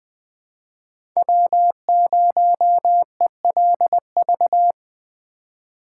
Morse code, heard as a keyed tone, about 20 wpm, 700 Hz. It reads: W0ELV
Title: Code Groups - 5 Letter:Numbers - 20wpm